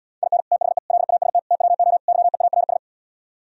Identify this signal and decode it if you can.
Morse code, heard as a keyed tone, about 77 wpm, 700 Hz. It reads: IN THE LAST TWO YEARS